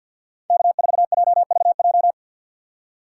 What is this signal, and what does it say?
Morse code, heard as a keyed tone, about 47 wpm, 700 Hz. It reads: K4JVJ
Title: USA Call Signs - 5 Characters - 6x Character Spacing - 50wpm